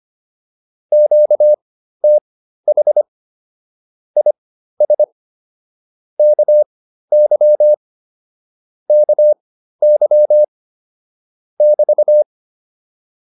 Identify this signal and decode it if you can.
Morse code, heard as a keyed tone, about 25 wpm, spaced as for 13 wpm, 600 Hz. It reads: QTH IS KY KY <BT>